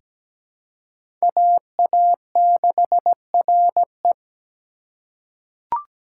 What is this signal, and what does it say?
Morse code, heard as a keyed tone, about 17 wpm, 700 Hz. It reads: AA6RE
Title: K1USN SST QSO Practice - 17wpm